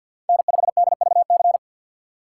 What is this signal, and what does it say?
Morse code, heard as a keyed tone, about 48 wpm, 700 Hz. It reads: N5DVC
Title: USA Call Signs - 2x Character Spacing - 50wpm